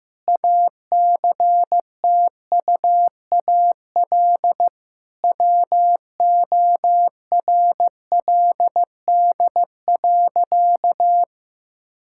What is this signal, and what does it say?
Morse code, heard as a keyed tone, about 15 wpm, 700 Hz. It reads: ACTUAL WORLD.
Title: Sentences from Top 1000 Words - Review Part 5 - 15wpm